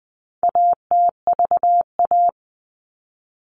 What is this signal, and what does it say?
Morse code, heard as a keyed tone, about 20 wpm, 700 Hz. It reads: ATVA
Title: All Cut Numbers - 4-Digit 20wpm